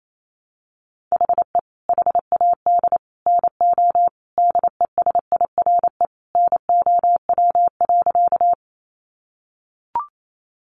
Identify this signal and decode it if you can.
Morse code, heard as a keyed tone, about 28 wpm, 700 Hz. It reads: HE HAD NO DESIRE NOW.
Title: Sentences from Top 1100 Words 28wpm